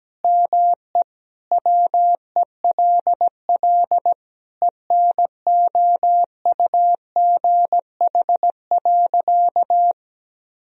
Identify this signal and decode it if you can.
Morse code, heard as a keyed tone, about 17 wpm, 700 Hz. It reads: ME WELL ENOUGH.
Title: Sentences from Top 400 Words and Abbreviations 17wpm